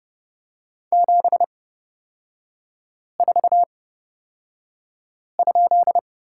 Morse code, heard as a keyed tone, about 30 wpm, spaced as for 5 wpm, 700 Hz. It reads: 74?
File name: Lesson 38e - Letter q - Three Characters 30wpm